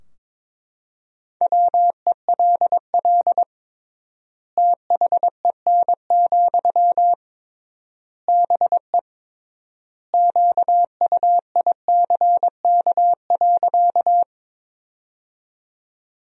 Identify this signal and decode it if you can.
Morse code, heard as a keyed tone, about 22 wpm, 700 Hz. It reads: WELL THEN, BE QUICK.
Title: 3x Word-Spacing - Sentences from Top 500 Words - 22wpm